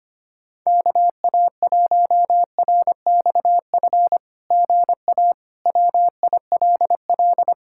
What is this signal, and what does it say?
Morse code, heard as a keyed tone, about 25 wpm, 700 Hz. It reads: KA1RXF GA WILL
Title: SOTA QSO Practice 25wpm